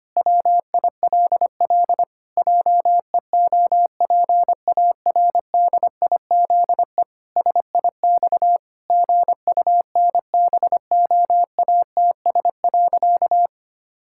Morse code, heard as a keyed tone, about 25 wpm, 700 Hz. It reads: WILL JEOPARDIZE SIX GUNBOATS.
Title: Pangram Challenge v2 - 25wpm